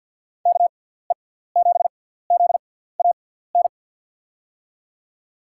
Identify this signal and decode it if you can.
Morse code, heard as a keyed tone, about 53 wpm, spaced as for 17 wpm, 700 Hz. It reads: KE7ZAN
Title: USA Call Signs - 6 Characters - 6x Character Spacing - 50wpm